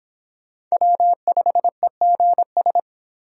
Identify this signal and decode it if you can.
Morse code, heard as a keyed tone, about 24 wpm, 700 Hz. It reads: W5EGS